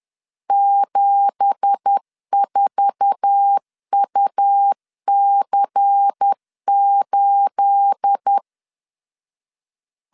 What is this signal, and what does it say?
Morse code, heard as a keyed tone, about 10 wpm, 800 Hz. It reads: 74UC8
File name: LEZIONI / 9 2867 10wpm 2733